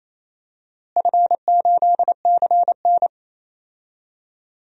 Morse code, heard as a keyed tone, about 27 wpm, 700 Hz. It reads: F8CN
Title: DXCC-100 Call Signs - 4 Characters - Repeated 3x - 28wpm